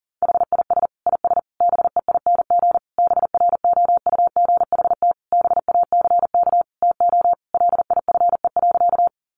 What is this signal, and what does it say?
Morse code, heard as a keyed tone, about 40 wpm, 700 Hz. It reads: HIS IS BEING BROUGHT BACK TO LIFE.
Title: Sentences from Top 600 Words 40wpm